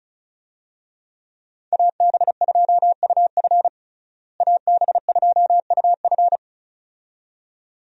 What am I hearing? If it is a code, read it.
Morse code, heard as a keyed tone, about 34 wpm, 700 Hz. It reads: AB2UF AB2UF
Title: USA Call Signs - 5 Characters - Repeated 2x - 35wpm